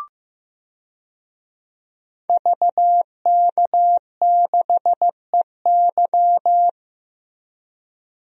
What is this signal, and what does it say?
Morse code, heard as a keyed tone, about 16 wpm, 700 Hz. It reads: VK6EY
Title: DXCC-100 Call Signs - 5 Characters - Repeated 2x - 15wpm